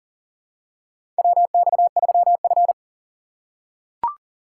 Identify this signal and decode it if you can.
Morse code, heard as a keyed tone, about 39 wpm, 700 Hz. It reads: WX3F
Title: USA Call Signs - 4 Characters - 4x Character Spacing - 40wpm